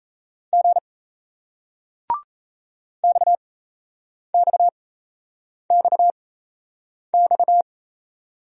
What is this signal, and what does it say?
Morse code, heard as a keyed tone, about 37 wpm, spaced as for 8 wpm, 700 Hz. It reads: G XXXX